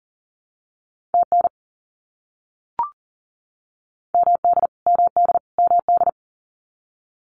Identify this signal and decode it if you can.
Morse code, heard as a keyed tone, about 40 wpm, 700 Hz. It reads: TN MD MD MD
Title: US States Repeated 3x 40wpm